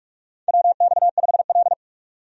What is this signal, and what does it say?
Morse code, heard as a keyed tone, about 46 wpm, 700 Hz. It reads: WX5L
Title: USA Call Signs - 4 Characters - 45wpm